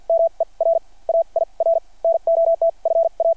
Morse code, heard as a keyed tone, about 53 wpm, spaced as for 39 wpm, 660 Hz. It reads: MER AIR NOT FA
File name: archived / CRSnet